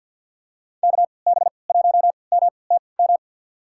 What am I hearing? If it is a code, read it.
Morse code, heard as a keyed tone, about 51 wpm, spaced as for 28 wpm, 700 Hz. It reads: KB1MTM
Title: USA Call Signs - 3x Character Spacing - 50wpm